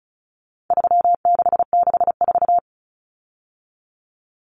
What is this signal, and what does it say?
Morse code, heard as a keyed tone, about 34 wpm, 700 Hz. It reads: 3664
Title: Common Cut Numbers - 4-Digit 35wpm